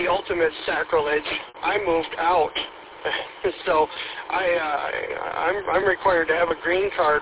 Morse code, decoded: none

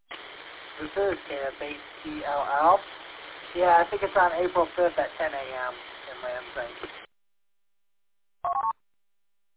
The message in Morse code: none

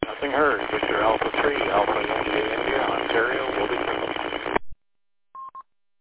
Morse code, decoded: none

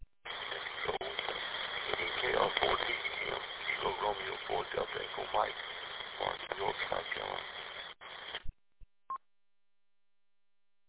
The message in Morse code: none

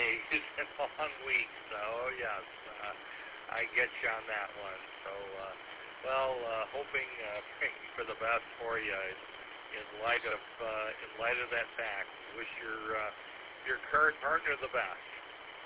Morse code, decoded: none

